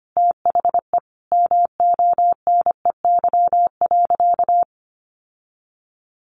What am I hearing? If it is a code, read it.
Morse code, heard as a keyed tone, about 25 wpm, 700 Hz. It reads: THE MONEY.
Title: Sentences from Top 1300 Words 25wpm